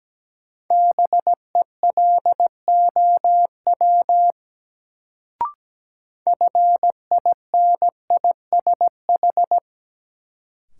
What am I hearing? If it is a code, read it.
Morse code, heard as a keyed tone, about 17 wpm, 700 Hz. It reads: BELOW FINISH
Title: Sentences from Top 1200 Words - New Word Drill Alt - 17wpm